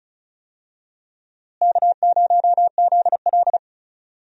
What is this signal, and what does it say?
Morse code, heard as a keyed tone, about 34 wpm, 700 Hz. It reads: K0ZL
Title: USA Call Signs - 4 Characters - Repeated 2x - 35wpm